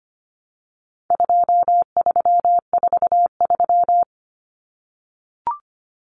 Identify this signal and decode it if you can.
Morse code, heard as a keyed tone, about 25 wpm, 700 Hz. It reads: 2343
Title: Numbers - 4-Digit 25wpm